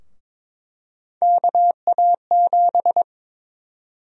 Morse code, heard as a keyed tone, about 22 wpm, 700 Hz. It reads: KA7
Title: USA Call Sign Prefixes - Repeated 2x - 22wpm